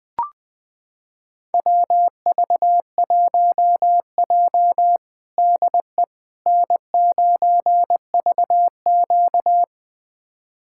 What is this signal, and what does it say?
Morse code, heard as a keyed tone, about 20 wpm, 700 Hz. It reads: WV1J DE N9VQ
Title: USA Callsign DE Callsign - 4 Characters - 20wpm